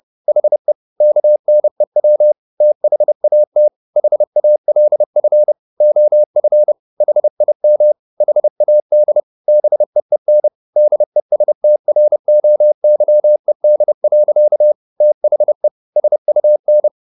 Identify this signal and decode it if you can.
Morse code, heard as a keyed tone, about 30 wpm, 600 Hz. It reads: HE KNEW THAT HALF OF HIM HAD BEEN DESTROYED. THE SUN